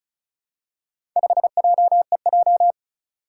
Morse code, heard as a keyed tone, about 36 wpm, 700 Hz. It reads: 5JEJ
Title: Four Challenging Letter-Number Combinations 35wpm